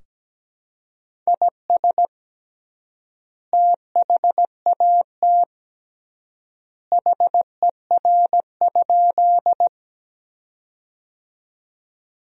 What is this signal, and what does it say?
Morse code, heard as a keyed tone, about 17 wpm, 700 Hz. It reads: IS THAT HER?
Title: 3x Word-Spacing - Sentences from Top 100 Words - 17wpm